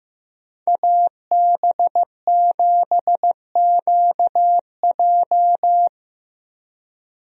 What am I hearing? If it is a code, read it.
Morse code, heard as a keyed tone, about 15 wpm, 700 Hz. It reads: AB7QJ
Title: USA Call Signs - 5 Characters - Repeated 2x - 15wpm